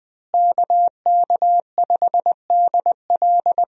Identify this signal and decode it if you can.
Morse code, heard as a keyed tone, about 20 wpm, 700 Hz. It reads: KK5DL
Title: USA Call Signs - 5 Characters - Repeated 3x - 20wpm